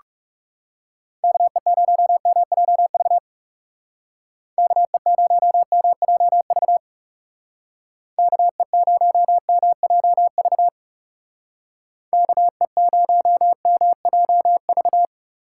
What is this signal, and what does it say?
Morse code, heard as a keyed tone, about 37 wpm, 700 Hz. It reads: KE0MJV KE0MJV KE0MJV KE0MJV
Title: Speed Racing - USA Call Signs 30wpm